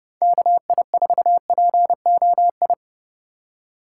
Morse code, heard as a keyed tone, about 28 wpm, 700 Hz. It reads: KI4POI